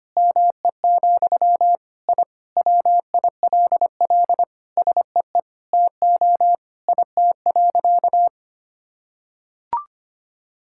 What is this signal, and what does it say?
Morse code, heard as a keyed tone, about 25 wpm, 700 Hz. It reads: ME, I WILL SEE TO IT.
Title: Sentences from Top 1000 Words - Review Part 5 - 25wpm